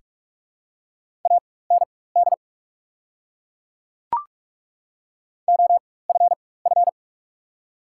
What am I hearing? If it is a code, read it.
Morse code, heard as a keyed tone, about 43 wpm, spaced as for 21 wpm, 690 Hz. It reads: AND OFF
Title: 4x Character Spacing - Top 200 Words - 45wpm